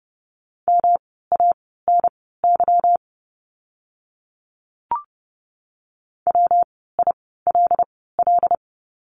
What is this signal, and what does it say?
Morse code, heard as a keyed tone, about 30 wpm, spaced as for 17 wpm, 700 Hz. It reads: MANY WILL
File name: course / 30:10wpm / 237 Lesson 38f - Letter q - Words 30wpm